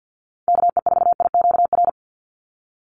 Morse code, heard as a keyed tone, about 50 wpm, 700 Hz. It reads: KE4IQR
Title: USA Call Signs 50wpm